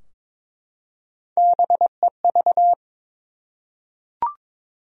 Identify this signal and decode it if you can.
Morse code, heard as a keyed tone, about 22 wpm, 700 Hz. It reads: BEV
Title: USA Common Names - Max 4 Letters - 22wpm